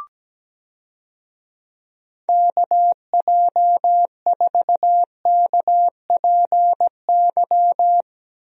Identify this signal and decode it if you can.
Morse code, heard as a keyed tone, about 17 wpm, 700 Hz. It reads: KJ4KPY
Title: USA Call Signs - 6 Characters - Repeated 2x - 17wpm